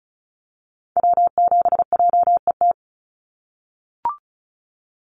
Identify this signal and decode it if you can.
Morse code, heard as a keyed tone, about 34 wpm, 700 Hz. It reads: W7JET
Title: NA SOTA Activator Callsigns 35wpm